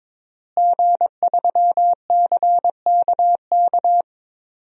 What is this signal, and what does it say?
Morse code, heard as a keyed tone, about 22 wpm, 700 Hz. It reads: G3CKK
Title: DXCC-100 Call Signs - 5 Characters - Repeated 2x - 22wpm